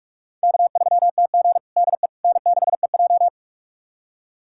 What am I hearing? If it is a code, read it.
Morse code, heard as a keyed tone, about 46 wpm, 700 Hz. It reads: K3TG DE N6EJ
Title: USA Callsign DE Callsign - 4 Characters - 45wpm